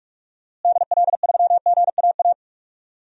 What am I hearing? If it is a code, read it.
Morse code, heard as a keyed tone, about 41 wpm, 700 Hz. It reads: DL3GAA
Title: DXCC-100 Call Signs - 6 Characters - Repeated 3x - 45wpm